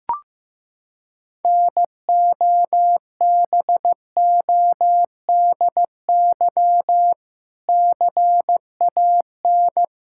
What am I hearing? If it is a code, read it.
Morse code, heard as a keyed tone, about 15 wpm, 700 Hz. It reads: NOBODY CAN
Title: Sentences from Top 1600 Words 15wpm